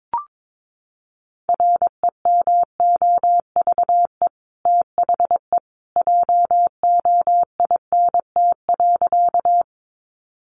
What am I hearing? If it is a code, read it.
Morse code, heard as a keyed tone, about 22 wpm, 700 Hz. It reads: REMOVE THE JOINT.
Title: Sentences from Top 1300 Words 22wpm